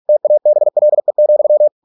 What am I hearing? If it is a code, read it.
Morse code, heard as a keyed tone, about 46 wpm, 600 Hz. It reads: TABLE,